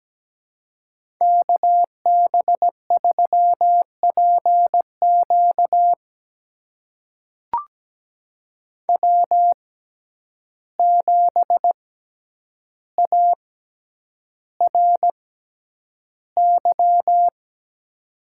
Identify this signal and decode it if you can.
Morse code, heard as a keyed tone, about 17 wpm, spaced as for 9 wpm, 700 Hz. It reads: KB3PQ W7ARY
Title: USA Call Signs - 5 Characters - 6x Character Spacing - 17wpm